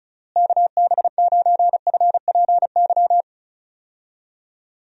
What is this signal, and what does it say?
Morse code, heard as a keyed tone, about 36 wpm, 700 Hz. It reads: KB9FPY
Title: USA Call Signs - 6 Characters - Repeated 3x - 35wpm